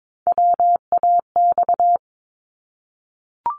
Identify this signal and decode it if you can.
Morse code, heard as a keyed tone, about 22 wpm, 700 Hz. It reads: WAX
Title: Multi-letter Phonemes 22wpm